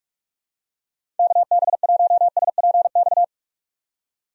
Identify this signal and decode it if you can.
Morse code, heard as a keyed tone, about 46 wpm, 700 Hz. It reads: KB1SPX
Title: USA Call Signs - 6 Characters - 45wpm